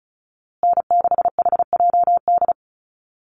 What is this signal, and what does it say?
Morse code, heard as a keyed tone, about 34 wpm, 700 Hz. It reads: N6HJD
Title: USA Call Signs Repeated 3x 35wpm